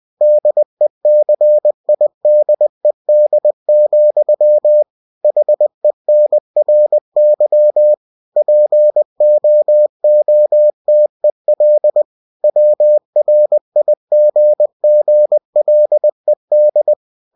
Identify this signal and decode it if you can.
Morse code, heard as a keyed tone, about 20 wpm, 600 Hz. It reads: DECIDED, HENRY POOTEL WRIGGLED